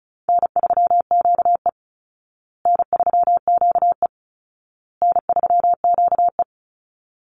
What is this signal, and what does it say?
Morse code, heard as a keyed tone, about 34 wpm, 700 Hz. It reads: N3QE N3QE N3QE
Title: Easy Call Signs Repeated 3x - 35wpm